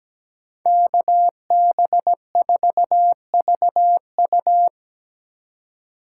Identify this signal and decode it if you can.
Morse code, heard as a keyed tone, about 17 wpm, 700 Hz. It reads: KB4VU
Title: K1USN SST QSO Practice - 17wpm